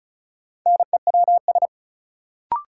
Morse code, heard as a keyed tone, about 36 wpm, 700 Hz. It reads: NEWS
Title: Sentences from Top 800 Words - New Word Drill - 35wpm